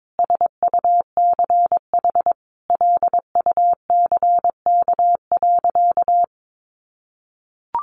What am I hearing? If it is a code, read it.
Morse code, heard as a keyed tone, about 22 wpm, 700 Hz. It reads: SUCH LUCK.
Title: Sentences from Top 1500 Words 22wpm